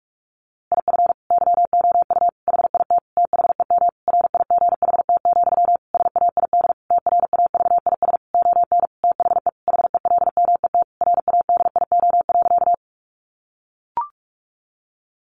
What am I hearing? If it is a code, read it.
Morse code, heard as a keyed tone, about 44 wpm, 700 Hz. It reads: IF YOU HIT THEM RIGHT, SAID TRAVIS ON THE HELMET RADIO.
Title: A Sound Of Thunder 45wpm